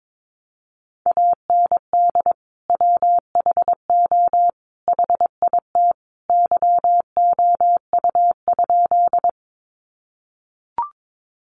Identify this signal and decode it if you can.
Morse code, heard as a keyed tone, about 22 wpm, 700 Hz. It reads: AND WHO HIT YOU?
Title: Sentences from Top 1200 Words 22wpm